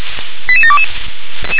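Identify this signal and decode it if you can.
Five-tone selective call (ZVEI), heard as quick runs of five tones, tones of about 70 ms. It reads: 88722
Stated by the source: Five-tone ZVEI selective call 88692 received on an SDR.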